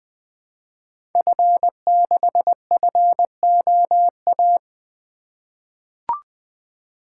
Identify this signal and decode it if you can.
Morse code, heard as a keyed tone, about 20 wpm, 700 Hz. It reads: F6FOA
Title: Code Groups - 5 Letter:Numbers - 20wpm